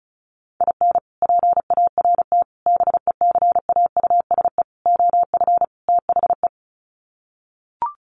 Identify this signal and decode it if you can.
Morse code, heard as a keyed tone, about 34 wpm, 700 Hz. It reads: IN PART BECAUSE OF THE
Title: Sets of 5 Words Encore2 35wpm